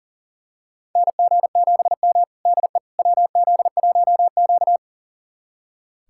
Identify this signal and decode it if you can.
Morse code, heard as a keyed tone, about 40 wpm, 700 Hz. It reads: NG7M DE WZ1Q
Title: USA Callsign DE Callsign - 4 Characters - 40wpm